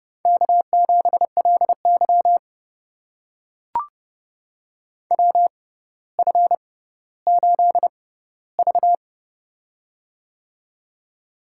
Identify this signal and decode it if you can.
Morse code, heard as a keyed tone, about 29 wpm, spaced as for 15 wpm, 700 Hz. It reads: K7LY WF8V
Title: USA Call Signs - 4 Characters - 6x Character Spacing - 30wpm